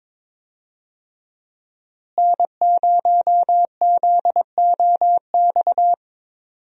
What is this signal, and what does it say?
Morse code, heard as a keyed tone, about 22 wpm, 700 Hz. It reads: N0ZOX
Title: USA Call Signs - 5 Characters - Repeated 2x - 22wpm